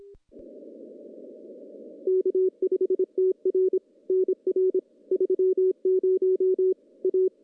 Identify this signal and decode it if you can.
Morse code, heard as a keyed tone, about 26 wpm, 370 Hz. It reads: K5TR NR 30 A
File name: contests / CW-SS-k5tr- -1500